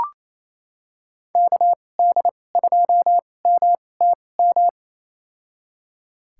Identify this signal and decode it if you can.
Morse code, heard as a keyed tone, about 28 wpm, spaced as for 20 wpm, 700 Hz. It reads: KD2MTM
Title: USA Call Signs - 6 Characters - 2x Character Spacing - 28wpm